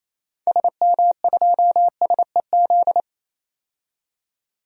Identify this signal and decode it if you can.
Morse code, heard as a keyed tone, about 28 wpm, 700 Hz. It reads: SM2SEZ
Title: DXCC-100 Call Signs - 6 Characters - 28wpm